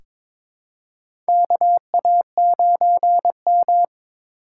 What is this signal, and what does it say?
Morse code, heard as a keyed tone, about 23 wpm, 700 Hz. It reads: KA9M
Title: USA Call Signs - 4 Characters - 3x Character Spacing - 22wpm